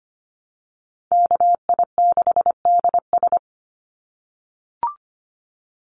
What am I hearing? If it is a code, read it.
Morse code, heard as a keyed tone, about 25 wpm, 700 Hz. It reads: KI6DS
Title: NA SOTA Chaser Callsigns 25wpm